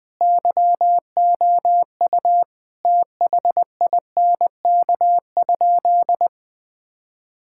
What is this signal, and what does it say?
Morse code, heard as a keyed tone, about 20 wpm, 700 Hz. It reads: YOU THINK?
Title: Sentences with 2-Word Phrases - Spoken Beginning - 20wpm